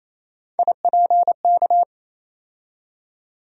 Morse code, heard as a keyed tone, about 28 wpm, 700 Hz. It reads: IPK
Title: USA Call Sign Postfixes - 3 Letters 28wpm